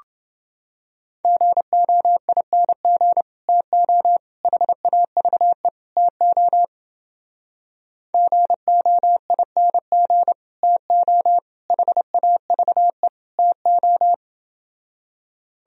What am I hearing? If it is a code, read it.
Morse code, heard as a keyed tone, about 30 wpm, 700 Hz. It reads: GOING TO HAVE TO GOING TO HAVE TO